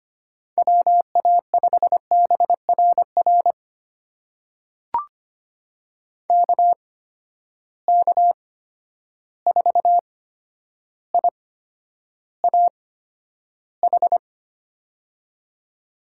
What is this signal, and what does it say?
Morse code, heard as a keyed tone, about 25 wpm, spaced as for 11 wpm, 700 Hz. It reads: WA5BRR KK4IAH
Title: USA Call Signs - 6 Characters - 8x Character Spacing - 25wpm